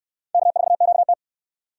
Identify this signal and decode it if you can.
Morse code, heard as a keyed tone, about 68 wpm, 700 Hz. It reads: K4QT